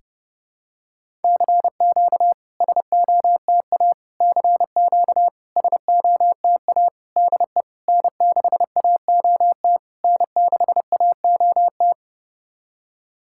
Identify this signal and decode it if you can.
Morse code, heard as a keyed tone, about 30 wpm, 700 Hz. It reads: CQ SOTA CQ SOTA DE N6AOT N6AOT
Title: SOTA QSO Practice 30wpm